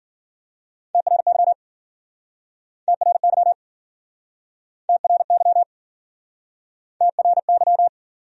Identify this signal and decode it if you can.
Morse code, heard as a keyed tone, about 52 wpm, 700 Hz. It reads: TRY TRY TRY TRY